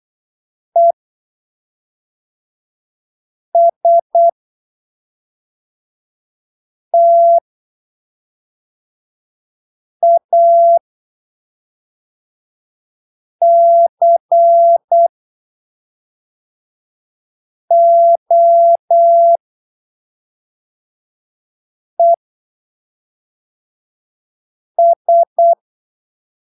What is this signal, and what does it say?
Morse code, heard as a keyed tone, about 8 wpm, spaced as for 3 wpm, 680 Hz. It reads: ESTACOES
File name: arquivos / texto / texto 03 08ppm